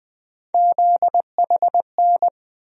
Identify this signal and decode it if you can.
Morse code, heard as a keyed tone, about 20 wpm, 700 Hz. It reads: ZHN
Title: USA Call Sign Postfixes - 3 Letters 20wpm